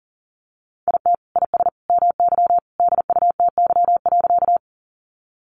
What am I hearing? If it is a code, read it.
Morse code, heard as a keyed tone, about 40 wpm, 700 Hz. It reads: IT IS MY DUTY.
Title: Sentences from Top 1500 Words 40wpm